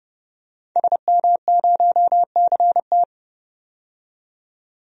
Morse code, heard as a keyed tone, about 31 wpm, 700 Hz. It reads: SM0CT
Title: DXCC-100 Call Signs - 5 Characters - 30wpm